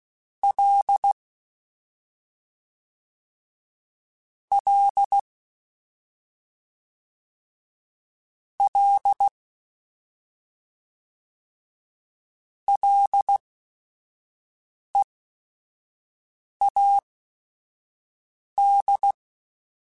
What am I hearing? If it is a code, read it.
Morse code, heard as a keyed tone, about 16 wpm, spaced as for 5 wpm, 800 Hz. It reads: L L L LEAD
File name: Ls / Single L m3s / Single L @5